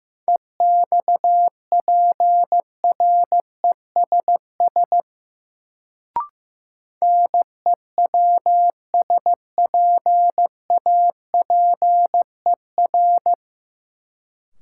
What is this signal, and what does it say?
Morse code, heard as a keyed tone, about 15 wpm, 700 Hz. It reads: EXPRESS NEWSPAPER
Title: Sentences from Top 1100 Words - New Word Drill Alt - 15wpm